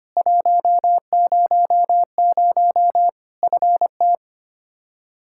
Morse code, heard as a keyed tone, about 25 wpm, 700 Hz. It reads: 100 FT
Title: Practice QSO - Exchange 2 - Info - 25wpm